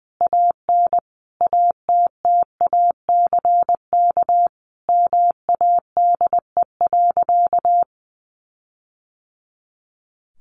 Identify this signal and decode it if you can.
Morse code, heard as a keyed tone, about 20 wpm, 700 Hz. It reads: AN ATTACK MADE.
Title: Sentences from Top 1500 Words 20wpm